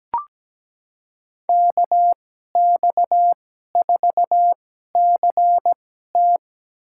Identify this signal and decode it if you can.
Morse code, heard as a keyed tone, about 17 wpm, spaced as for 12 wpm, 700 Hz. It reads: KX4CT
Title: USA Call Signs - 5 Characters - 2x Character Spacing - 17wpm